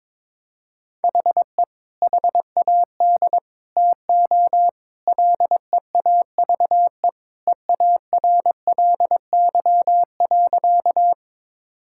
Morse code, heard as a keyed tone, about 21 wpm, 700 Hz. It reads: HE HAD TO LEAVE EARLY.
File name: practice / 22wpm / Sentences with 3-Word Phrases - 22wpm